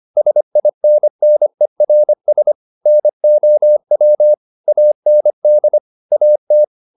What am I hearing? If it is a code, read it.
Morse code, heard as a keyed tone, about 25 wpm, 600 Hz. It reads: SINNERS NOW AND AT
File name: the old man and the sea part 1 25wpm